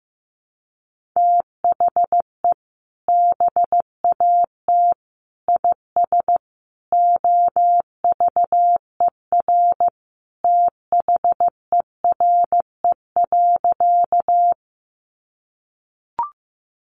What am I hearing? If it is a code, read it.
Morse code, heard as a keyed tone, about 15 wpm, 700 Hz. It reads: THE BAT IS OVER THERE.
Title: Sentences from Top 1500 Words 15wpm